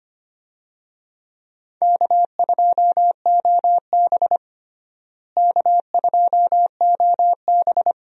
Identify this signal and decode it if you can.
Morse code, heard as a keyed tone, about 25 wpm, 700 Hz. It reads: K2OB K2OB
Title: USA Call Signs - 4 Characters - Repeated 2x - 25wpm